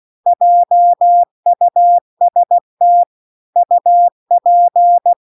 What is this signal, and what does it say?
Morse code, heard as a keyed tone, about 16 wpm, 700 Hz. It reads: JUST UP
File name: CW-The-Big-Twoheaded-River-2-16WPM